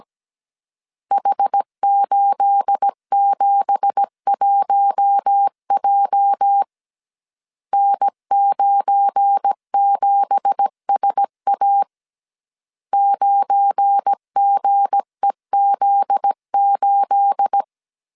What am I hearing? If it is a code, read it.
Morse code, heard as a keyed tone, about 17 wpm, 800 Hz. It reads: H871J N97SA 9GEZ8
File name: CW / DET / 1 9 1500C S C 16wpm 2909